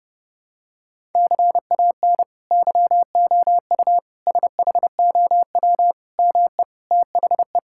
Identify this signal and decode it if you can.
Morse code, heard as a keyed tone, about 30 wpm, 700 Hz. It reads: CAN YOU SHOW ME THE
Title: Sentences from Top 1000 Words - Review Part 6 - 30wpm